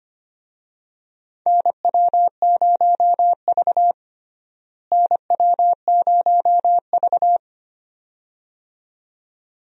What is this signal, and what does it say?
Morse code, heard as a keyed tone, about 25 wpm, 700 Hz. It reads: NW0V NW0V
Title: USA Call Signs - 4 Characters - Repeated 2x - 25wpm